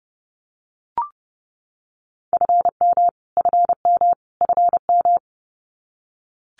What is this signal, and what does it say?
Morse code, heard as a keyed tone, about 30 wpm, 700 Hz. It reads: FM FM FM
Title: Abbreviations Repeated 3x - 30wpm